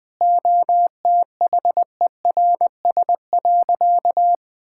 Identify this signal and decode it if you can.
Morse code, heard as a keyed tone, about 20 wpm, 700 Hz. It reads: OTHERS.
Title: Sentences with 6-Word Phrases - 20wpm